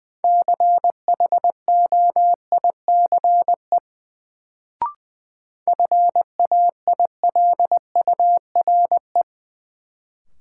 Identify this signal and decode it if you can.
Morse code, heard as a keyed tone, about 20 wpm, 700 Hz. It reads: CHOICE FAILURE
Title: Sentences from Top 900 Words - New Word Drill - 20wpm